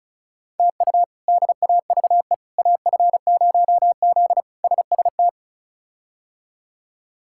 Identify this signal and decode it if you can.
Morse code, heard as a keyed tone, about 34 wpm, 700 Hz. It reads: TU DAVE AF0Z SST
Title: K1USN SST QSO Practice - 35wpm